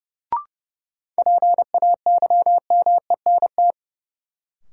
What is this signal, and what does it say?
Morse code, heard as a keyed tone, about 30 wpm, 700 Hz. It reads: PAYMENT
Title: Sentences from Top 1200 Words - New Word Drill Alt - 30wpm